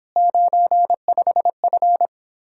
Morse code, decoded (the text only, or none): 95F